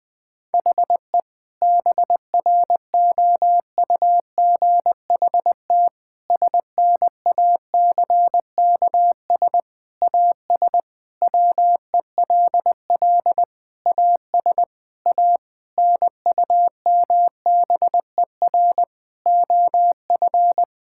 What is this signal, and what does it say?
Morse code, heard as a keyed tone, about 20 wpm, 700 Hz. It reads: HE BROUGHT SNACKS AS WELL AS A NUMBER OF
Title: Sentences with 6-Word Phrases - 20wpm